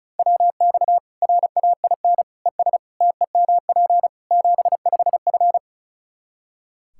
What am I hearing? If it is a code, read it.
Morse code, heard as a keyed tone, about 36 wpm, 700 Hz. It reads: WX RAIN ES TEMP 75F
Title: Practice QSO - Exchange 2 - Info - 35wpm